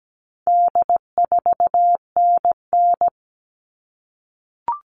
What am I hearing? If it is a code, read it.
Morse code, heard as a keyed tone, about 17 wpm, 700 Hz. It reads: D4NN
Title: All Cut Numbers - 4-Digit 17wpm